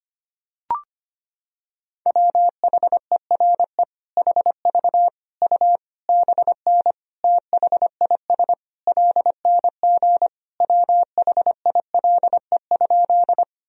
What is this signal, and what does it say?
Morse code, heard as a keyed tone, about 25 wpm, 700 Hz. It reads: WHERE HV U BN THIS LNG WHILE?
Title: Sentences from Top 300 Words and Abbreviations 25wpm